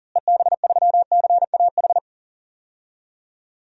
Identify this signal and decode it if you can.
Morse code, heard as a keyed tone, about 40 wpm, 700 Hz. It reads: EB3CAH
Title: DXCC-100 Call Signs - 6 Characters - Repeated 2x - 40wpm